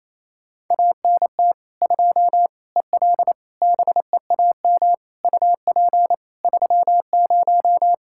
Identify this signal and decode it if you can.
Morse code, heard as a keyed tone, about 28 wpm, 700 Hz. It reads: ANT 2 EL BEAM UP 30
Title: Practice QSO - Exchange 2 - Info - 28wpm